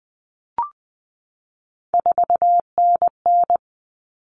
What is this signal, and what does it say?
Morse code, heard as a keyed tone, about 20 wpm, 700 Hz. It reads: 4NN
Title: All Cut Numbers - 3-Digit 20wpm